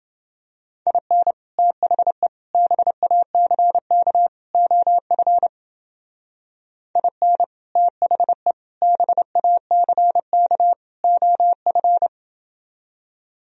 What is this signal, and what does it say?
Morse code, heard as a keyed tone, about 29 wpm, 700 Hz. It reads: IN THE BACK OF IN THE BACK OF